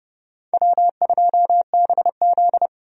Morse code, heard as a keyed tone, about 31 wpm, 700 Hz. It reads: W2BZ
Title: USA Call Signs - 4 Characters - 30wpm